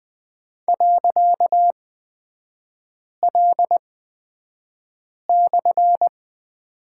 Morse code, heard as a keyed tone, about 20 wpm, spaced as for 5 wpm, 700 Hz. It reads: .L/
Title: 255 Lesson 41e - Period - Three Characters 20wpm